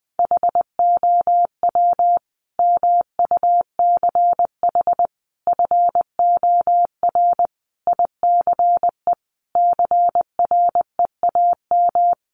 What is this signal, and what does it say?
Morse code, heard as a keyed tone, about 20 wpm, 700 Hz. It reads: HOW MUCH FOR ICE CREAM
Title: Sentences from Top 1500 Words 20wpm